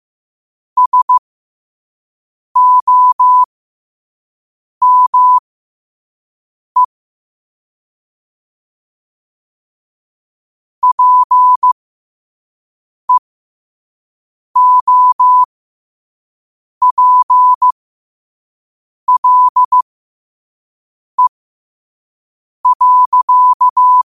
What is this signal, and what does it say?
Morse code, heard as a keyed tone, about 15 wpm, spaced as for 5 wpm, 1000 Hz. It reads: SOME PEOPLE.
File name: qotdfiles / 5wpm